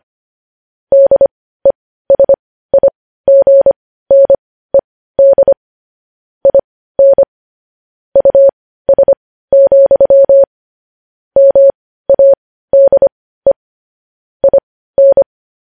Morse code, heard as a keyed tone, about 25 wpm, spaced as for 15 wpm, 550 Hz. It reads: DESIGNED IN US, MADE IN
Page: news headlines in morse code at 15 wpm